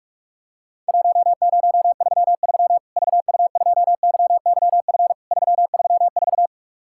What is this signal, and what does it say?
Morse code, heard as a keyed tone, about 46 wpm, 700 Hz. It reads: 1033 VU2YYF 334
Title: Worked All Europe DX Contest - QTC Exchange - 45wpm